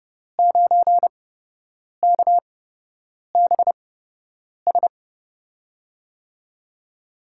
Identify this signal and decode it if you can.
Morse code, heard as a keyed tone, about 30 wpm, spaced as for 8 wpm, 700 Hz. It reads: 9KBS
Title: USA Call Signs - 5 Characters - 8x Character Spacing - 30wpm